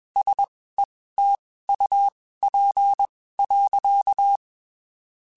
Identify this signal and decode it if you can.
Morse code, heard as a keyed tone, about 21 wpm, spaced as for 15 wpm, 800 Hz. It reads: SETUP.
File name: (spaced) / QSOs Spaced-mp3s / WD0BGP 1 @20 sp